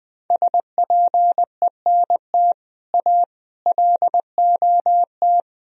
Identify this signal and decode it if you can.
Morse code, heard as a keyed tone, about 20 wpm, 700 Hz. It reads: SPENT A LOT
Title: Speed-Racing - Sets of 5 Words 17wpm